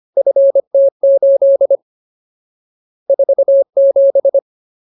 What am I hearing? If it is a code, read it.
Morse code, heard as a keyed tone, about 25 wpm, 550 Hz. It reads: FT8 47
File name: Ultimate Word List / 25WPM Ultimate